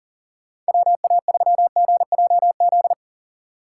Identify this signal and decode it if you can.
Morse code, heard as a keyed tone, about 40 wpm, 700 Hz. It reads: WA3GJZ